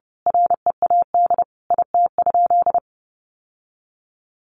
Morse code, heard as a keyed tone, about 30 wpm, 700 Hz. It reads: READ IT?
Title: Sentences from Top 1200 Words 30wpm